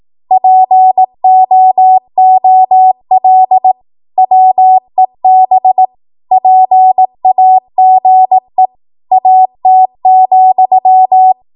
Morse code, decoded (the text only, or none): POOL WEB PAGE AT,